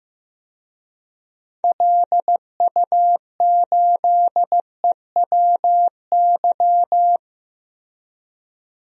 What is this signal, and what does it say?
Morse code, heard as a keyed tone, about 15 wpm, 700 Hz. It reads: LU8EWY
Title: DXCC-100 Call Signs - 6 Characters - Repeated 2x - 15wpm